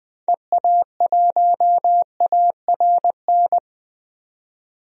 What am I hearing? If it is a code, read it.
Morse code, heard as a keyed tone, about 21 wpm, 700 Hz. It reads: EA1ARN
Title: DXCC-100 Call Signs - 6 Characters - 20wpm